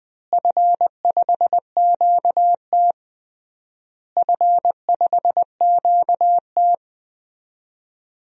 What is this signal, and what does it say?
Morse code, heard as a keyed tone, about 20 wpm, 700 Hz. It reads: F5QT F5QT
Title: DXCC-100 Call Signs - 4 Characters - Repeated 3x - 20wpm